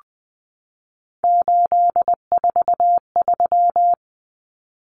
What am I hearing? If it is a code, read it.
Morse code, heard as a keyed tone, about 20 wpm, 700 Hz. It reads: 843
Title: Common Cut Numbers - 3-Digit 20wpm